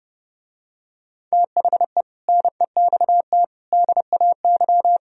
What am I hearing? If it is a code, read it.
Morse code, heard as a keyed tone, about 30 wpm, 700 Hz. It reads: THE NEXT DAY